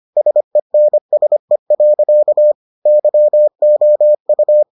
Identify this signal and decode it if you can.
Morse code, heard as a keyed tone, about 25 wpm, 600 Hz. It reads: SENSE. YOU